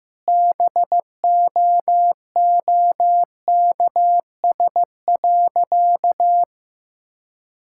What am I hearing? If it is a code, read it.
Morse code, heard as a keyed tone, about 15 wpm, 700 Hz. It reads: BOOKS.
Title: Sentences from Top 1000 Words - Review Part 5 - 15wpm